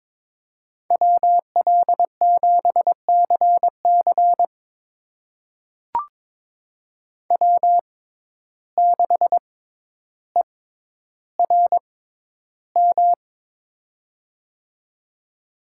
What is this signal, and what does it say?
Morse code, heard as a keyed tone, about 21 wpm, spaced as for 11 wpm, 700 Hz. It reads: WL7CC W6ERM
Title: USA Call Signs - 5 Characters - 6x Character Spacing - 22wpm